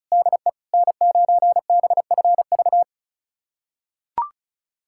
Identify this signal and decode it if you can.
Morse code, heard as a keyed tone, about 34 wpm, 700 Hz. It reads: DE N9BFV
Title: USA Callsign DE Callsign - 5 Characters - 35wpm